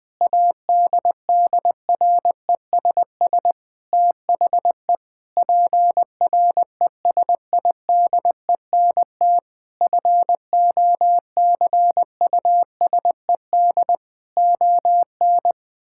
Morse code, heard as a keyed tone, about 21 wpm, 700 Hz. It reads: ADDRESS THE PRESIDENT FOCUSED ON
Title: Sentences with 7-Word Phrases - Spoken Beginning - 20wpm